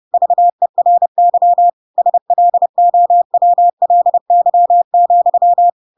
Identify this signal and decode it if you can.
Morse code, heard as a keyed tone, about 30 wpm, 700 Hz. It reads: VERY SLOWLY,
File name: Pinocchio / Pinocchio ch6 30 30 ik1hge